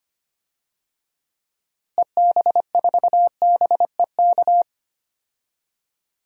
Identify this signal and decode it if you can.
Morse code, heard as a keyed tone, about 25 wpm, 700 Hz. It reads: EB4BEK
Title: DXCC-100 Call Signs - 6 Characters - Repeated 3x - 25wpm